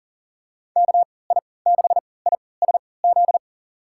Morse code, heard as a keyed tone, about 40 wpm, spaced as for 23 wpm, 700 Hz. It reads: KI6ISZ
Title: USA Call Signs - 3x Character Spacing - 40wpm